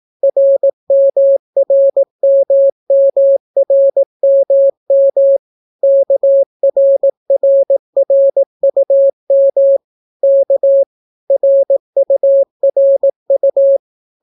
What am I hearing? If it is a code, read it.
Morse code, heard as a keyed tone, about 18 wpm, 550 Hz. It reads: RMRMMRMM KRRRUM K RURU